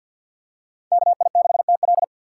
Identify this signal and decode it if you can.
Morse code, heard as a keyed tone, about 51 wpm, 700 Hz. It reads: KI6TL
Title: USA Call Signs - 2x Character Spacing - 50wpm